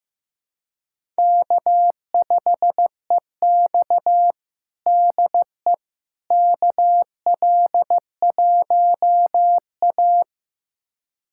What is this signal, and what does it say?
Morse code, heard as a keyed tone, about 15 wpm, 700 Hz. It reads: K5EX DE KL1A
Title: USA Callsign DE Callsign - 4 Characters - 15wpm